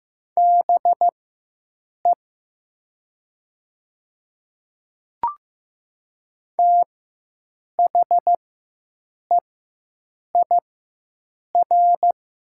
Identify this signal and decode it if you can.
Morse code, heard as a keyed tone, about 15 wpm, spaced as for 7 wpm, 700 Hz. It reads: BE THEIR
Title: 4x Character Spacing - Top 100 Words - 15wpm